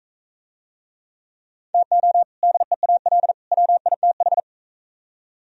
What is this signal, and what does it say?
Morse code, heard as a keyed tone, about 42 wpm, 700 Hz. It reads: TO DEAL WITH